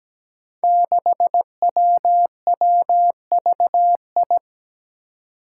Morse code, heard as a keyed tone, about 17 wpm, 700 Hz. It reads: 6WWVI
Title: Five Challenging Letter-Number Combinations 17wpm